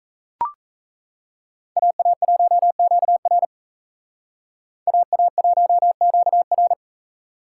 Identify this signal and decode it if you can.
Morse code, heard as a keyed tone, about 35 wpm, 700 Hz. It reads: AA1QR AA1QR